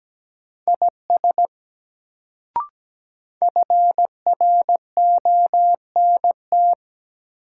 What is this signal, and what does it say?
Morse code, heard as a keyed tone, about 17 wpm, 700 Hz. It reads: IS FRONT
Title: Sets of 2 Words - New Word Drill - 17wpm